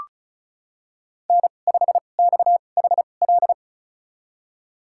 Morse code, heard as a keyed tone, about 34 wpm, spaced as for 25 wpm, 700 Hz. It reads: N5XHL
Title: USA Call Signs - 5 Characters - 2x Character Spacing - 35wpm